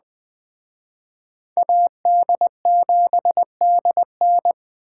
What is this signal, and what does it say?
Morse code, heard as a keyed tone, about 21 wpm, 700 Hz. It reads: AD7DN
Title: Speed Racing - USA 2x2 Call Signs 17wpm